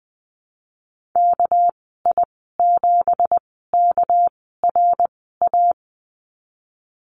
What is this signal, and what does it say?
Morse code, heard as a keyed tone, about 20 wpm, spaced as for 15 wpm, 700 Hz. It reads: KI7KRA
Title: 213 Lesson 34g - Number 8 - Call Signs 20wpm